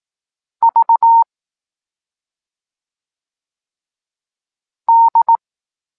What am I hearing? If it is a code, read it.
Morse code, heard as a keyed tone, about 18 wpm, 920 Hz. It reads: V D
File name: Tekst-23 4